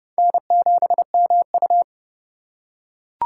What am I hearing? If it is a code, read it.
Morse code, heard as a keyed tone, about 30 wpm, 700 Hz. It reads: N7MU
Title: USA Call Signs - 4 Characters - 3x Character Spacing - 30wpm